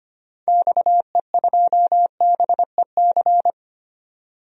Morse code, heard as a keyed tone, about 25 wpm, 700 Hz. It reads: XE2BEC
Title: DXCC-100 Call Signs - 6 Characters - 25wpm